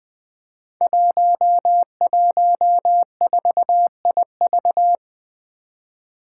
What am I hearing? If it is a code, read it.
Morse code, heard as a keyed tone, about 20 wpm, 700 Hz. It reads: 114IV
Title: Five Challenging Letter-Number Combinations 20wpm